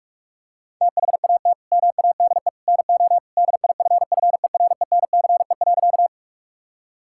Morse code, heard as a keyed tone, about 46 wpm, 690 Hz. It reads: THAT MADE NO DIFFERENCE.
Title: Sentences from Top 1000 Words - Review Part 5 - 45wpm